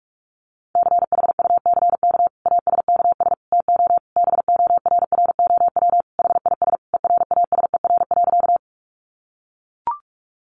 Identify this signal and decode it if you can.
Morse code, heard as a keyed tone, about 44 wpm, 700 Hz. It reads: CHUCK ASKS TO BORROW HIS ERASER.
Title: Additional Easy Phrases 45wpm